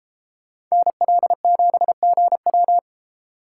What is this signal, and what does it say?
Morse code, heard as a keyed tone, about 35 wpm, 700 Hz. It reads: NL7GW